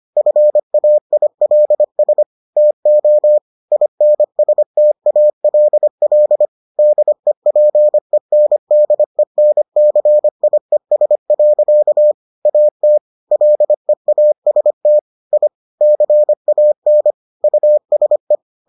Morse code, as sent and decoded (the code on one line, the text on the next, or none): ..-. .- .. .-.. ... / - --- / .. -. ... - .- .-.. .-.. / -.. .--. -. -.. -. -.-. .. ... .-.-.- / .- - / .-.. .- ... - / .. / -.-. .- -. / ..- ... .
FAILS TO INSTALL DEPENDENCIES. AT LEAST I CAN USE